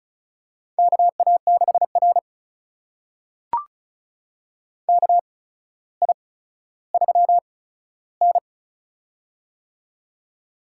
Morse code, heard as a keyed tone, about 36 wpm, spaced as for 15 wpm, 700 Hz. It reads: KA6R KI3N
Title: USA Call Signs - 4 Characters - 8x Character Spacing - 35wpm